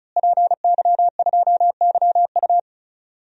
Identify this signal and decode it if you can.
Morse code, heard as a keyed tone, about 34 wpm, 700 Hz. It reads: PY2YU
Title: DXCC-100 Call Signs - 5 Characters - Repeated 3x - 35wpm